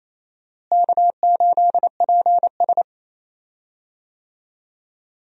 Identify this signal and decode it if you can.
Morse code, heard as a keyed tone, about 28 wpm, 700 Hz. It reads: K8PS
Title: USA Call Signs - 4 Characters - Repeated 3x - 28wpm